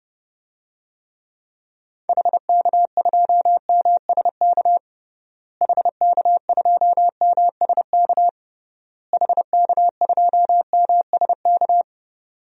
Morse code, heard as a keyed tone, about 30 wpm, 700 Hz. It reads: HK2MSK HK2MSK HK2MSK
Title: DXCC-100 Call Signs - 6 Characters - Repeated 3x - 30wpm